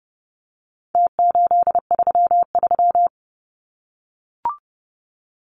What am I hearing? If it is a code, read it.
Morse code, heard as a keyed tone, about 30 wpm, 700 Hz. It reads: T833
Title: Numbers - Contest Style - 4-Digit 30wpm